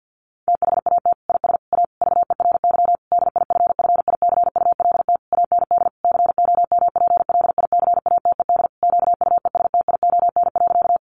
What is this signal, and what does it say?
Morse code, heard as a keyed tone, about 50 wpm, 700 Hz. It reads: THAT IS A VERY DIFFICULT AND COMPLICATED QUESTION.
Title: Sentences from Top 1200 Words 50wpm